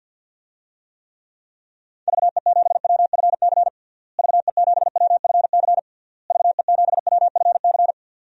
Morse code, heard as a keyed tone, about 52 wpm, 700 Hz. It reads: VE7WFC VE7WFC VE7WFC
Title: DXCC-100 Call Signs - 6 Characters - Repeated 3x - 50wpm